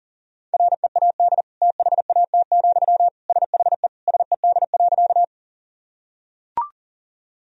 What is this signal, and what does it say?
Morse code, heard as a keyed tone, about 40 wpm, 700 Hz. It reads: READ THAT, SHE SED.
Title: Sentences from Top 200 Words and Abbreviations 40wpm